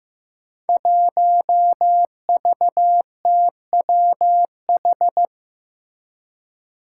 Morse code, heard as a keyed tone, about 15 wpm, 700 Hz. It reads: 1VTWH
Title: Five Challenging Letter-Number Combinations 15wpm